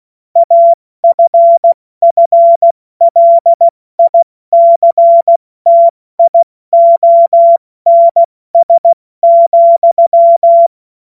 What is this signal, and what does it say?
Morse code, heard as a keyed tone, about 16 wpm, 680 Hz. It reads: AFFLICTIONS,